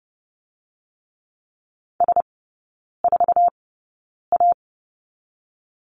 Morse code, heard as a keyed tone, about 30 wpm, spaced as for 9 wpm, 700 Hz. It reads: S4A
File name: 081 Lesson 14e - Number 2 - Three Characters 30wpm